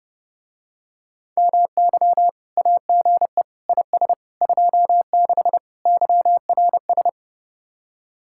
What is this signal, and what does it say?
Morse code, heard as a keyed tone, about 31 wpm, 700 Hz. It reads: MY AGE IS 26 YRS